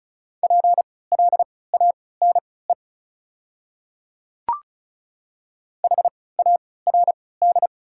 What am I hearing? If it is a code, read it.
Morse code, heard as a keyed tone, about 34 wpm, spaced as for 20 wpm, 700 Hz. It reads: PLANE HARD
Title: 3x Character Spacing - Top 500 Words - 35wpm